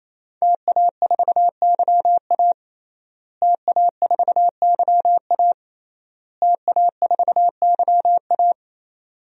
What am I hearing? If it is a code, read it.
Morse code, heard as a keyed tone, about 27 wpm, 700 Hz. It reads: TA4YA TA4YA TA4YA
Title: DXCC-100 Call Signs - 5 Characters - Repeated 3x - 28wpm